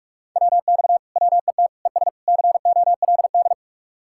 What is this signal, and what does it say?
Morse code, heard as a keyed tone, about 58 wpm, 700 Hz. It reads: WX WET ES COLD